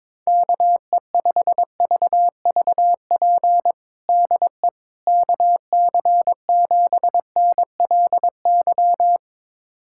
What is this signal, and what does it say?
Morse code, heard as a keyed tone, about 22 wpm, 700 Hz. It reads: KE5VVP DE KC7NLY
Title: USA Callsign DE Callsign 22wpm